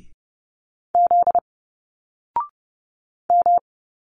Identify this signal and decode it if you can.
Morse code, heard as a keyed tone, about 30 wpm, 700 Hz. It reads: Z M
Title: Single Letters 30wpm